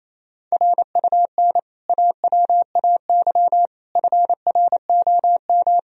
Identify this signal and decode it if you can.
Morse code, heard as a keyed tone, about 28 wpm, 700 Hz. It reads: RUN AWAY FROM
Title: Sentences from Top 1000 Words - Review Part 1 - 28wpm